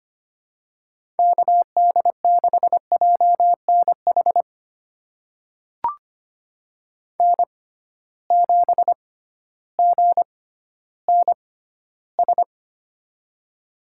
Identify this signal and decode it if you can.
Morse code, heard as a keyed tone, about 25 wpm, spaced as for 14 wpm, 700 Hz. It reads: KD6JNH N7GNS
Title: USA Call Signs - 6x Character Spacing - 25wpm